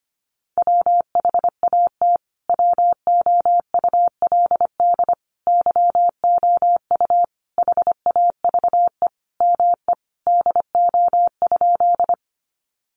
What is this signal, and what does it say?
Morse code, heard as a keyed tone, about 25 wpm, 700 Hz. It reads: WHAT WOULD YOU HAVE ME DO?
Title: Sentences from Top 200 Words 25wpm